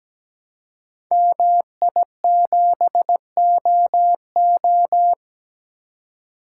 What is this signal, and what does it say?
Morse code, heard as a keyed tone, about 17 wpm, 700 Hz. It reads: MI7OO
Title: Code Groups - 5 Letter:Numbers - 17wpm